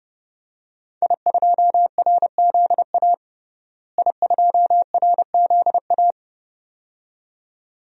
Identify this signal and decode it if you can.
Morse code, heard as a keyed tone, about 32 wpm, 700 Hz. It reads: I2RZA I2RZA
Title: DXCC-100 Call Signs - 5 Characters - Repeated 2x - 30wpm